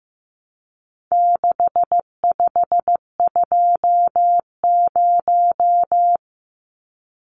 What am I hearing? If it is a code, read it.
Morse code, heard as a keyed tone, about 15 wpm, 700 Hz. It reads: 6520
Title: Numbers - 4-Digit 15wpm